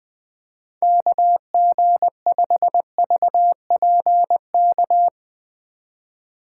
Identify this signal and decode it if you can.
Morse code, heard as a keyed tone, about 20 wpm, 700 Hz. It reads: KG5VPK
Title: USA Call Signs - 2x Character Spacing - 20wpm